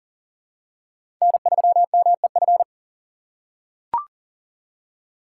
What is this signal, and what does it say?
Morse code, heard as a keyed tone, about 40 wpm, 700 Hz. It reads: N3MEF
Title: USA Call Signs - 5 Characters - 2x Character Spacing - 40wpm